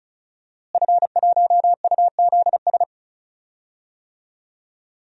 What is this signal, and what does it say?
Morse code, heard as a keyed tone, about 34 wpm, 700 Hz. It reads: F1UZS
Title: DXCC-100 Call Signs - 5 Characters - Repeated 3x - 35wpm